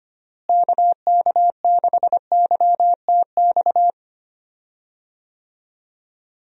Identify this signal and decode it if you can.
Morse code, heard as a keyed tone, about 25 wpm, 700 Hz. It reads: KK6YTX
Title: USA Call Signs - 6 Characters - Repeated 2x - 25wpm